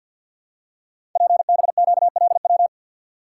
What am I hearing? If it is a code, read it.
Morse code, heard as a keyed tone, about 50 wpm, 700 Hz. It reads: PBQLW
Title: Code Groups - 5 Letters - 50wpm